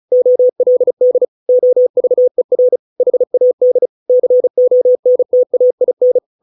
Morse code, heard as a keyed tone, about 36 wpm, 500 Hz. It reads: OLD OVER HAD CONTAIN